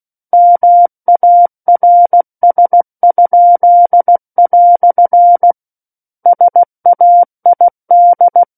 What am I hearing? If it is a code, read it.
Morse code, heard as a keyed tone, about 16 wpm, 700 Hz. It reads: MARS?" SAID
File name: Book One / 07-Chapter 7 - 16 WPM